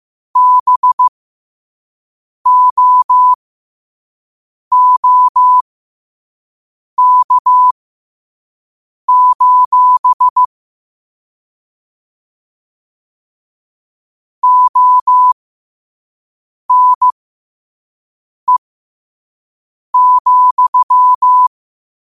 Morse code, heard as a keyed tone, about 15 wpm, spaced as for 5 wpm, 1000 Hz. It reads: BOOK: ONE,
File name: qotdfiles / 5wpm